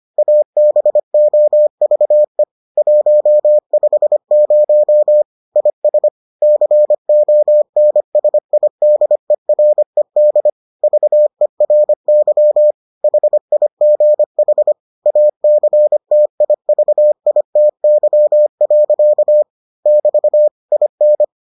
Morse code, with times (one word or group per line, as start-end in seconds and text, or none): 0.2-2.4 ABOVE
2.8-5.2 150
5.6-6.1 IS
6.4-10.5 CONSIDERED
10.8-12.7 VERY
13.0-14.7 HIGH
15.1-19.4 ACTIVITY.
19.9-20.5 <BT>
20.7-21.2 IN